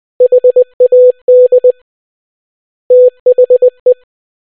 Morse code, heard as a keyed tone, about 20 wpm, 500 Hz. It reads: HAD THE